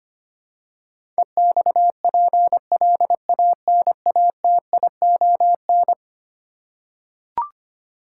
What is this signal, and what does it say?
Morse code, heard as a keyed tone, about 25 wpm, 700 Hz. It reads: EXPLANATION
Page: morse-code-ninja-practice-25wpm